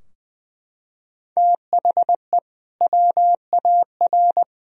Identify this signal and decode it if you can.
Morse code, heard as a keyed tone, about 20 wpm, 700 Hz. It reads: THE WAR
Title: Sentences from Top 1000 Words - Review Part 2 - 20wpm